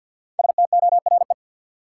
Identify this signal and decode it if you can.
Morse code, heard as a keyed tone, about 46 wpm, 700 Hz. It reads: STORE
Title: Sentences from Top 1000 Words - Review Part 5 - New Word Drill - 50wpm